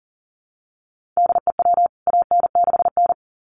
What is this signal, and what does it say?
Morse code, heard as a keyed tone, about 40 wpm, 700 Hz. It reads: DE WAN6N
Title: CWT Exchange 40wpm